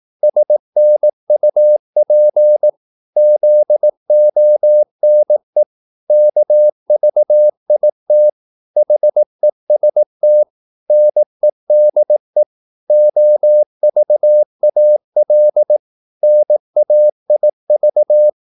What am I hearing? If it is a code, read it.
Morse code, heard as a keyed tone, about 18 wpm, 600 Hz. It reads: SNUP ZONE KVIT HEST NEDE OVAL NAIV
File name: Korte ord DK 18wpm